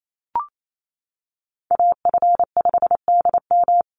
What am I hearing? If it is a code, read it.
Morse code, heard as a keyed tone, about 28 wpm, 700 Hz. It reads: AF5DM
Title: Call Signs - 28wpm